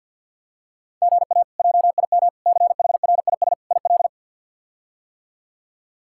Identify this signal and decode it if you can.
Morse code, heard as a keyed tone, about 53 wpm, 700 Hz. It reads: GA JIM CHRIS IL
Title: K1USN SST QSO Practice - 50wpm